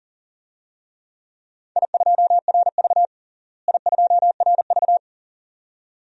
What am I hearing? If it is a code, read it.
Morse code, heard as a keyed tone, about 40 wpm, 700 Hz. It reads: I2RV I2RV
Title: DXCC-100 Call Signs - 4 Characters - Repeated 2x - 40wpm